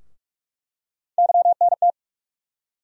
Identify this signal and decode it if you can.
Morse code, heard as a keyed tone, about 42 wpm, 700 Hz. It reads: YNT
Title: USA Call Sign Postfixes - 3 Letters 45wpm